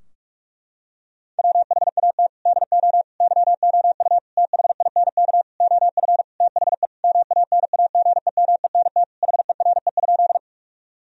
Morse code, heard as a keyed tone, about 46 wpm, 700 Hz. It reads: WHAT DO YOU THINK OF THE MANAGEMENT HERE?
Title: Sentences from Top 1000 Words - Review Part 4 - 45wpm